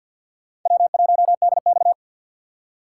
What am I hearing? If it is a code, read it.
Morse code, heard as a keyed tone, about 52 wpm, 700 Hz. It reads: W1DX
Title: K1USN SST QSO Practice - 50wpm